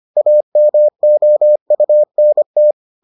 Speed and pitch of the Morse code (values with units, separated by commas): 25 wpm, 600 Hz